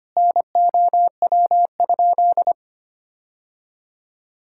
Morse code, heard as a keyed tone, about 25 wpm, 700 Hz. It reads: NOW?
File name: practice / 25wpm / Sentences with 3-Word Phrases - 25wpm